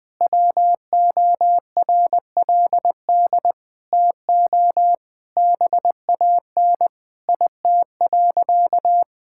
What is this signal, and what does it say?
Morse code, heard as a keyed tone, about 19 wpm, 700 Hz. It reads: WORLD TO BAN IT.
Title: Sentences with 7-Word Phrases - Spoken Beginning - 20wpm